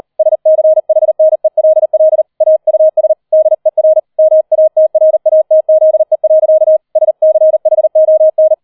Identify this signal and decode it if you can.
Morse code, heard as a keyed tone, about 38 wpm, 630 Hz. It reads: SCHNELL AUS DER MATRATZE. SCHON
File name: 40wpm